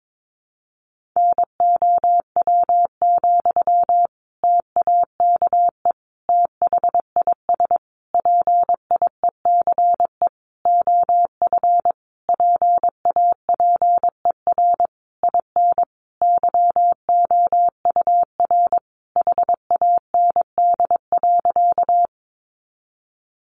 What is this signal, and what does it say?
Morse code, heard as a keyed tone, about 22 wpm, 700 Hz. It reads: NOW, TAKE THIS PIECE OF PAPER IN YOUR HAND.
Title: Sentences from Top 400 Words 22wpm